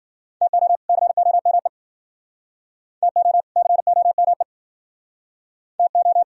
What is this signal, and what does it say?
Morse code, heard as a keyed tone, about 57 wpm, 700 Hz. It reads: TO COME TO COME TO